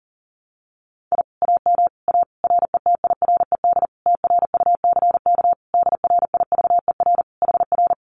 Morse code, heard as a keyed tone, about 40 wpm, 700 Hz. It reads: I AM A RETIRED TRUCK DRIVER HR
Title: Common QSO Phrases Encore 40wpm